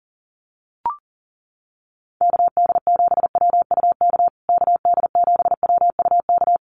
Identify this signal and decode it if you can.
Morse code, heard as a keyed tone, about 40 wpm, 700 Hz. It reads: KD7WUK KD7WUK
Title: USA Call Signs Repeated 3x 40wpm